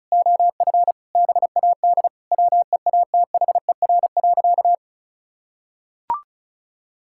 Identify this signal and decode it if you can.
Morse code, heard as a keyed tone, about 36 wpm, 690 Hz. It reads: OF BAD WEATHER.
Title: Sentences from Top 1000 Words - Review Part 6 - 35wpm